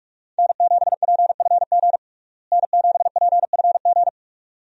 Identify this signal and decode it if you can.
Morse code, heard as a keyed tone, about 46 wpm, 700 Hz. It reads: N7PFG N7PFG
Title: USA Call Signs - 5 Characters - Repeated 2x - 45wpm